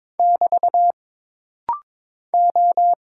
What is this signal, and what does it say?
Morse code, heard as a keyed tone, about 22 wpm, 700 Hz. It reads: <BT> O
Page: This rapid-fire timing is geared to push your ICR (Instant Character Recognition) skills.